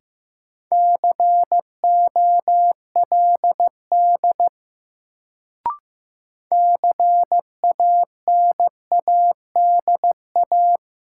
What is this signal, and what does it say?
Morse code, heard as a keyed tone, about 15 wpm, 700 Hz. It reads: COLD CANADA
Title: Sets of 5 Words - New Word Drill - 15wpm